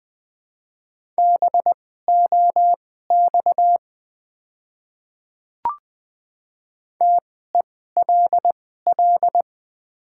Morse code, heard as a keyed tone, about 20 wpm, spaced as for 14 wpm, 700 Hz. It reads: BOX TELL
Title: 2x Character Spacing - Top 500 Words - 20wpm